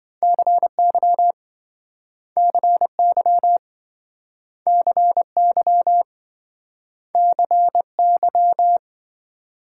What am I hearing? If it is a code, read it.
Morse code, heard as a keyed tone, about 27 wpm, 700 Hz. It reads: CY CY CY CY